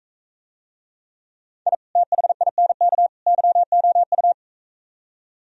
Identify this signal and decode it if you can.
Morse code, heard as a keyed tone, about 45 wpm, 700 Hz. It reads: I THINK YOU